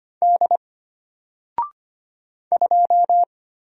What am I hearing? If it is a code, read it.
Morse code, heard as a keyed tone, about 25 wpm, 700 Hz. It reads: D 2
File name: Single-Letter-Number-25wpm